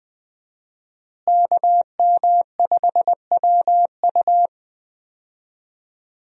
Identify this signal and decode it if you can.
Morse code, heard as a keyed tone, about 20 wpm, 700 Hz. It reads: KM5WU
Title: USA Call Signs - 5 Characters - Repeated 2x - 20wpm